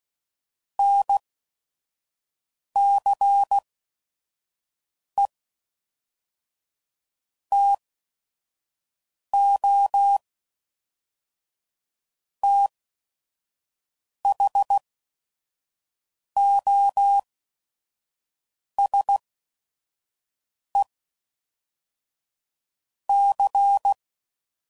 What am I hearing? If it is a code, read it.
Morse code, heard as a keyed tone, about 16 wpm, 800 Hz. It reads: N C E T O T H O S E C